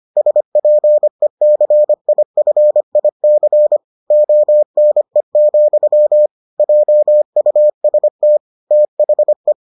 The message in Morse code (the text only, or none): SPECIFIC ONE, JUST THE